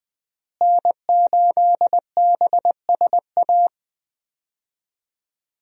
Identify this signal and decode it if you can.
Morse code, heard as a keyed tone, about 20 wpm, 700 Hz. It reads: N8BSA
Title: USA Call Signs - 5 Characters - Repeated 3x - 20wpm